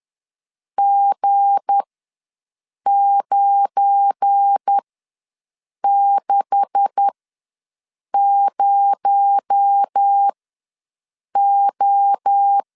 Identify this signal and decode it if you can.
Morse code, heard as a keyed tone, about 10 wpm, spaced as for 6 wpm, 800 Hz. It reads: G960O
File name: LEZIONI / 9 286 ws 10wpm 4032